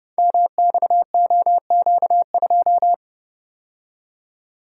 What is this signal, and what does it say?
Morse code, heard as a keyed tone, about 28 wpm, 700 Hz. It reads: MXOQ2
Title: Code Groups - 5 Letter:Numbers - 30wpm